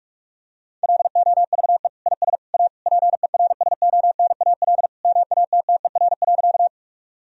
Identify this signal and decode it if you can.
Morse code, heard as a keyed tone, about 45 wpm, 700 Hz. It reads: LOVE IS A PERSONAL MATTER.
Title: Sentences from Top 1000 Words - Review Part 4 - 45wpm